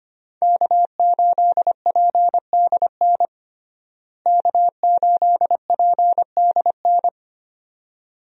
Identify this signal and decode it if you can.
Morse code, heard as a keyed tone, about 25 wpm, 700 Hz. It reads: K8PDN K8PDN
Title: USA Call Signs - 5 Characters - Repeated 2x - 25wpm